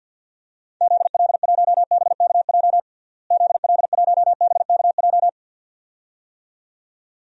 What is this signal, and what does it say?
Morse code, heard as a keyed tone, about 52 wpm, 700 Hz. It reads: ZL1BKJ ZL1BKJ
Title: DXCC-100 Call Signs - 6 Characters - Repeated 2x - 50wpm